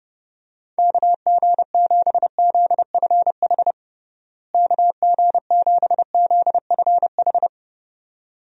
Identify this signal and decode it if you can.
Morse code, heard as a keyed tone, about 30 wpm, 700 Hz. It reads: KG7ZFH KG7ZFH
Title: USA Call Signs - 6 Characters - Repeated 2x - 30wpm